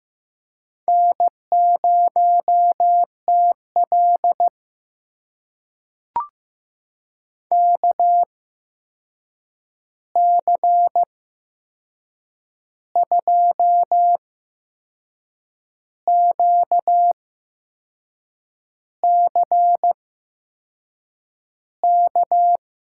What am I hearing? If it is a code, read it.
Morse code, heard as a keyed tone, about 15 wpm, spaced as for 6 wpm, 700 Hz. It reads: N0TL KC2QCK
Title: USA Call Signs - 8x Character Spacing - 15wpm